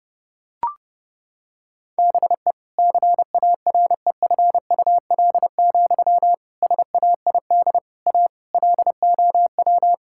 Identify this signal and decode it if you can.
Morse code, heard as a keyed tone, about 32 wpm, 700 Hz. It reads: BE CAREFUL, SAID A LOW
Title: Sentences from Top 1600 Words 30wpm